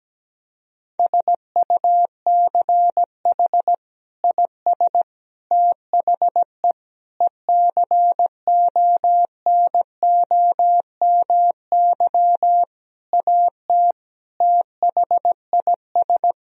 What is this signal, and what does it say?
Morse code, heard as a keyed tone, about 17 wpm, 700 Hz. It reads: SUCH IS THE ECONOMY AT THIS
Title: Sentences from Top 1000 Words - Review Part 3 - 17wpm